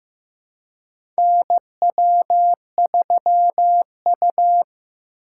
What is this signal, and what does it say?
Morse code, heard as a keyed tone, about 15 wpm, 700 Hz. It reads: NW3U
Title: USA Call Signs - 4 Characters - 4x Character Spacing - 15wpm